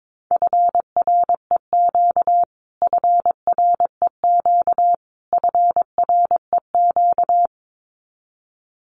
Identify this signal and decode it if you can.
Morse code, heard as a keyed tone, about 22 wpm, 700 Hz. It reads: FREQ FREQ FREQ
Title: Abbreviations Repeated 3x - 22wpm